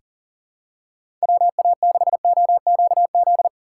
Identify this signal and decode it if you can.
Morse code, heard as a keyed tone, about 40 wpm, 700 Hz. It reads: WA6OQZ
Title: USA Call Signs - 6 Characters - 2x Character Spacing - 40wpm